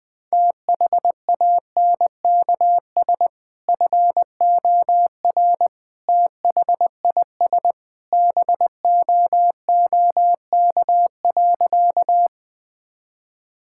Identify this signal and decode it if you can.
Morse code, heard as a keyed tone, about 20 wpm, 700 Hz. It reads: THANKS FOR THIS BOOK.
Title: Sentences with 2-Word Phrases - Spoken Beginning - 20wpm